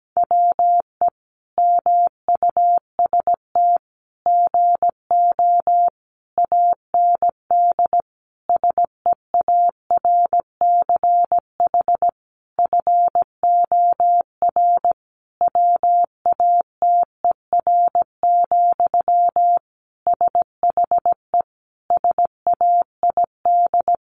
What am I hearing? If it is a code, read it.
Morse code, heard as a keyed tone, about 17 wpm, 700 Hz. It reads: WE MUST GO AND SEARCH FOR WATER, SHE SAID
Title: Sentences from Top 800 Words 17wpm